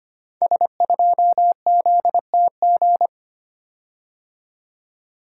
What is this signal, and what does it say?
Morse code, heard as a keyed tone, about 25 wpm, 700 Hz. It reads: S2ZTG
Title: Code Groups - 5 Letter:Numbers - 25wpm